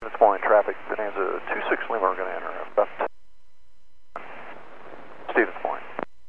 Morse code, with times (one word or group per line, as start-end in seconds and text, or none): none